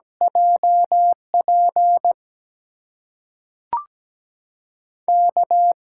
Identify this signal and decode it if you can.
Morse code, heard as a keyed tone, about 17 wpm, 700 Hz. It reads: JP K
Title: USA Call Signs - 8x Character Spacing - 17wpm